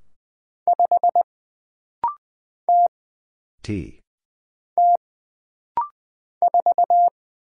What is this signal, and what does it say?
Morse code, heard as a keyed tone, about 20 wpm, 700 Hz. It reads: 5 T T 4
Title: Single Letter-Number - Rapid-Fire 20wpm